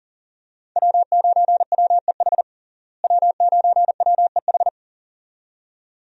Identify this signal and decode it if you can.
Morse code, heard as a keyed tone, about 37 wpm, 700 Hz. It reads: W9WEH W9WEH
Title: USA Call Signs - 5 Characters - Repeated 2x - 40wpm